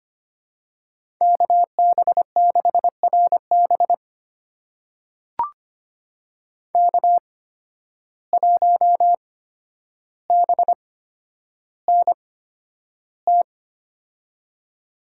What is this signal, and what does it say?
Morse code, heard as a keyed tone, about 25 wpm, spaced as for 11 wpm, 700 Hz. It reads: KB6RB K1BNT
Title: USA Call Signs - 5 Characters - 8x Character Spacing - 25wpm